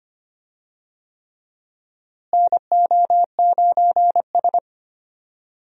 Morse code, heard as a keyed tone, about 25 wpm, 700 Hz. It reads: NO9S
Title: USA Call Signs - 4 Characters - Repeated 3x - 25wpm